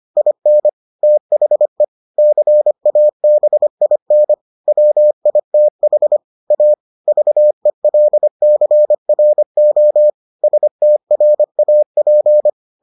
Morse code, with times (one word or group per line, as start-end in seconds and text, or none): 0.2-0.7 IN
1.0-1.8 THE
2.2-4.3 CABIN
4.7-6.2 WITH
6.5-6.7 A
7.1-10.1 VELCRO
10.4-12.5 STRAP